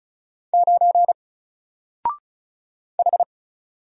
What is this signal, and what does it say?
Morse code, heard as a keyed tone, about 36 wpm, 700 Hz. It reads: 9 H
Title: Single Character (Letter:Number:Punctuation:Prosign) Alt 35wpm